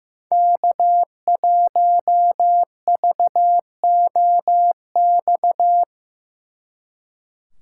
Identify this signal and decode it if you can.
Morse code, heard as a keyed tone, about 17 wpm, 700 Hz. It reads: K1VOX
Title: Speed Racing - USA 1x3 Call Signs 15wpm